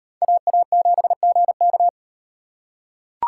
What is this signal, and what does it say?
Morse code, heard as a keyed tone, about 38 wpm, 700 Hz. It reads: AA7GK